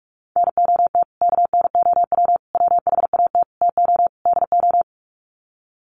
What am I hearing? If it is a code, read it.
Morse code, heard as a keyed tone, about 44 wpm, 700 Hz. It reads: NOT KNOW WHAT TO DO
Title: Sets of 5 Words 45wpm